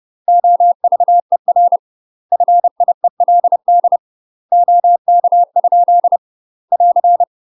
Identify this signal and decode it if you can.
Morse code, heard as a keyed tone, about 30 wpm, 700 Hz. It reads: OVER FIELD OK? <AR>
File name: UK MASTER QSOs TRAINING / 30WPM / UK MASTER QSOs 0016